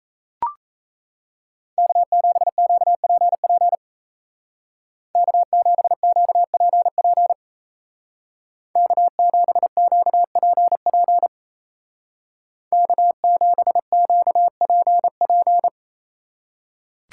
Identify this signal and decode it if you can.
Morse code, heard as a keyed tone, about 37 wpm, 700 Hz. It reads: K7QPP K7QPP K7QPP K7QPP